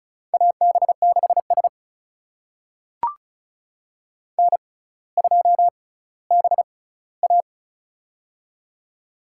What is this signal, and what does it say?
Morse code, heard as a keyed tone, about 36 wpm, spaced as for 18 wpm, 700 Hz. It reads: AB6S N2BA
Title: USA Call Signs - 4 Characters - 6x Character Spacing - 35wpm